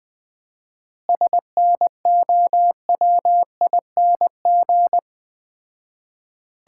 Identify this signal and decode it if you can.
Morse code, heard as a keyed tone, about 20 wpm, 700 Hz. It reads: SNOWING
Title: Common QSO Phrases - 4x Word-Spacing - 20wpm